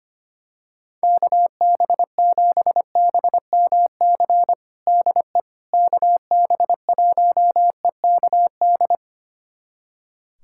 Morse code, heard as a keyed tone, about 25 wpm, 700 Hz. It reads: KB7BMC DE KB1EKD
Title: USA Callsign DE Callsign - 6 Characters - 25wpm